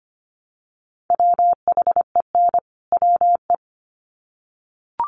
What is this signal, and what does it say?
Morse code, heard as a keyed tone, about 25 wpm, 700 Hz. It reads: WHEN WE
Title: Sets of 2 Words 25wpm